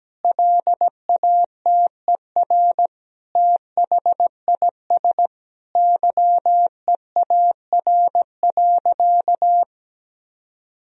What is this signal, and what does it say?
Morse code, heard as a keyed tone, about 17 wpm, 700 Hz. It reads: LATER THIS YEAR.
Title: Sentences from Top 1000 Words - Review Part 3 - 17wpm